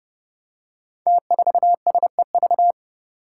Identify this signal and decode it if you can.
Morse code, heard as a keyed tone, about 30 wpm, 700 Hz. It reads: T4SEV
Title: Five Challenging Letter-Number Combinations 30wpm